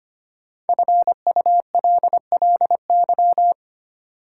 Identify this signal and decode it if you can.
Morse code, heard as a keyed tone, about 25 wpm, 700 Hz. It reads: FULLY
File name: 5 Letter Words - First 3 Letters Given - 25wpm